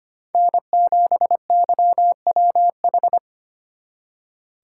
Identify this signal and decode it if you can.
Morse code, heard as a keyed tone, about 33 wpm, 700 Hz. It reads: N7YWH